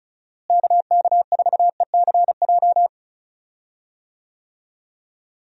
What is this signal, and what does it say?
Morse code, heard as a keyed tone, about 36 wpm, 700 Hz. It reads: KK4ECJ
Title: USA Call Signs - 6 Characters - Repeated 3x - 35wpm